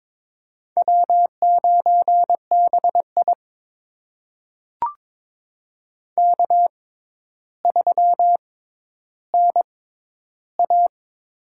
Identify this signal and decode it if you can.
Morse code, heard as a keyed tone, about 22 wpm, spaced as for 11 wpm, 700 Hz. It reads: W9BI K3NA
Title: USA Call Signs - 6x Character Spacing - 22wpm